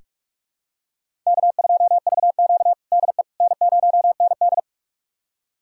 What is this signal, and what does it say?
Morse code, heard as a keyed tone, about 43 wpm, 700 Hz. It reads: K2VQ DE N0ND
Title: USA Callsign DE Callsign - 4 Characters - 45wpm